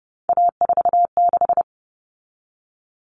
Morse code, heard as a keyed tone, about 30 wpm, 700 Hz. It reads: A46
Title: Common Cut Numbers - 3-Digit 30wpm